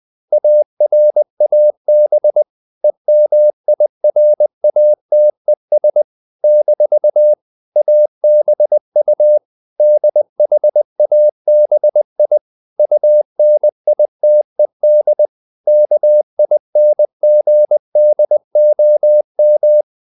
Morse code, heard as a keyed tone, about 20 wpm, 600 Hz. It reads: ARAB EMIRATES - ABU DHABI UNITED KINGDOM
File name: CW / Practice / NationalCapitals20